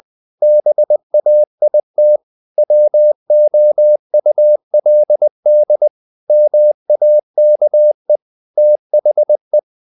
Morse code, heard as a keyed tone, about 20 wpm, 600 Hz. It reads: BAIT WOULD MAKE THE
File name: the old man and the sea / the old man and the sea part 1 20wpm